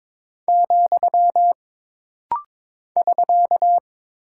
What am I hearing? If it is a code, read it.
Morse code, heard as a keyed tone, about 22 wpm, 700 Hz. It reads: , <SK>
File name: practice / 22wpm / Single Punctuation-Prosign - Rapid-Fire - 22wpm